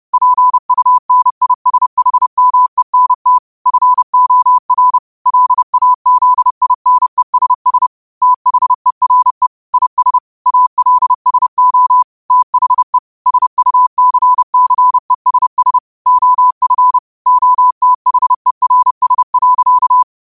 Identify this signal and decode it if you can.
Morse code, heard as a keyed tone, about 30 wpm, 1000 Hz. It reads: PUNISHMENT FOR LAZINESS THERE IS ALSO THE SUCCESS OF OTHERS.
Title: 30 WPM morse code quotes for Thu, 29 Feb 2024